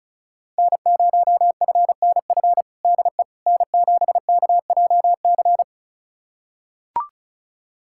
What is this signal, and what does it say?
Morse code, heard as a keyed tone, about 34 wpm, 690 Hz. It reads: N0FNF DE N7KJC
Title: USA Callsign DE Callsign - 5 Characters - 35wpm